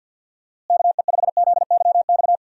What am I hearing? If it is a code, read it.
Morse code, heard as a keyed tone, about 52 wpm, 700 Hz. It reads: KE5ZYX
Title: USA Call Signs - 6 Characters - Repeated 2x - 50wpm